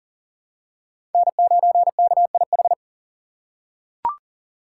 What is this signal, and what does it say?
Morse code, heard as a keyed tone, about 40 wpm, 700 Hz. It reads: N9KIH
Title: USA Call Signs - 5 Characters - 3x Character Spacing - 40wpm